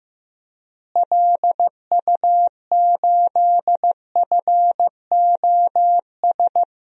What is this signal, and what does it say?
Morse code, heard as a keyed tone, about 15 wpm, 700 Hz. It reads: LU8FOS
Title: DXCC-100 Call Signs - 6 Characters - Repeated 3x - 15wpm